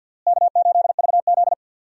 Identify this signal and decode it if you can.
Morse code, heard as a keyed tone, about 52 wpm, 700 Hz. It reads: K8VZ
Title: USA Call Signs - 4 Characters - Repeated 3x - 50wpm